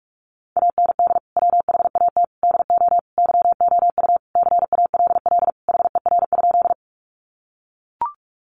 Morse code, heard as a keyed tone, about 44 wpm, 700 Hz. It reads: AND WHAT DO YOU CALL HER?
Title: Sentences from Top 100 Words 45wpm